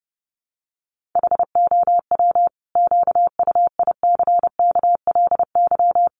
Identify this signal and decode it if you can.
Morse code, heard as a keyed tone, about 30 wpm, 700 Hz. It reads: HOW QUICKLY
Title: Sentences from Top 700 Words 30wpm